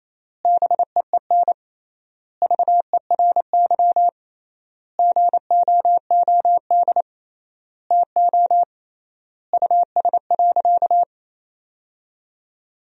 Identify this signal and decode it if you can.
Morse code, heard as a keyed tone, about 27 wpm, 700 Hz. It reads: BEEN VERY GOOD TO US.
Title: 3x Word-Spacing - Sentences from Top 200 Words - 28wpm